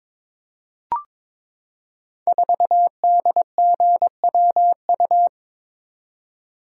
Morse code, heard as a keyed tone, about 22 wpm, 700 Hz. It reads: 4DGWU
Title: Code Groups - 5 Letter:Numbers - 22wpm